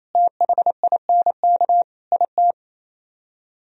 Morse code, heard as a keyed tone, about 33 wpm, 700 Hz. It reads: THINK IT
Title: Speed-Racing - Sets of 4 Words 28wpm